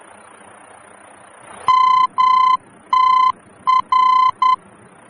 Morse code, heard as a keyed tone, about 10 wpm, 1020 Hz. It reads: MTR
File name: germany / mx / MTR-D-VOR